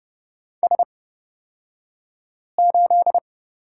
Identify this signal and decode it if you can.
Morse code, heard as a keyed tone, about 31 wpm, spaced as for 5 wpm, 700 Hz. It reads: S8
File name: Lesson 38d - Letter q - Two Characters 30wpm